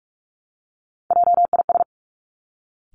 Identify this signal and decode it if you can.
Morse code, heard as a keyed tone, about 46 wpm, 700 Hz. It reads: JIS
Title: Three Challenging Letter-Number Combinations 45wpm